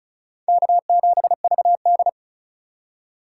Morse code, heard as a keyed tone, about 36 wpm, 700 Hz. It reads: K7VD
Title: USA Call Signs - 2x Character Spacing - 35wpm